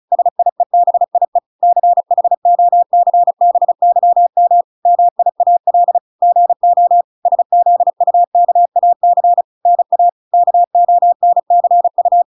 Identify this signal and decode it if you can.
Morse code, heard as a keyed tone, about 34 wpm, 700 Hz. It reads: SIEBIE CHOCBYM MIAL GO SZUKAC NA KONCU